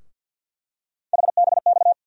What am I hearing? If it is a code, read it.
Morse code, heard as a keyed tone, about 51 wpm, 700 Hz. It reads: HBX
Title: USA Call Sign Postfixes - 3 Letters 50wpm